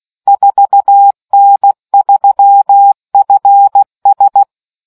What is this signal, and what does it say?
Morse code, heard as a keyed tone, about 16 wpm, 800 Hz. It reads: 4N3FS